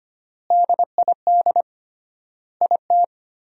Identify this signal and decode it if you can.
Morse code, heard as a keyed tone, about 25 wpm, 700 Hz. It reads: DID IT
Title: 3x Word-Spacing - Sentences from Top 200 Words - 25wpm